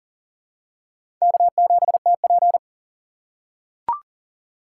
Morse code, heard as a keyed tone, about 40 wpm, 700 Hz. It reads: K7TP
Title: USA Call Signs - 4 Characters - 3x Character Spacing - 40wpm